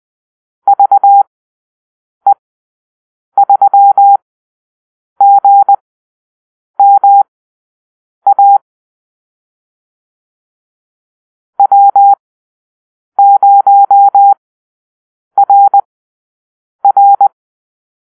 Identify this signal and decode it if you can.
Morse code, heard as a keyed tone, about 20 wpm, spaced as for 7 wpm, 800 Hz. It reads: VE3GMA W0RR
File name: arrl / Call-Sign Drill-7